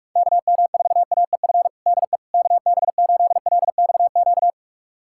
Morse code, heard as a keyed tone, about 46 wpm, 700 Hz. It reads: KM4AEF DE KB8LXQ
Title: USA Callsign DE Callsign - 6 Characters - 45wpm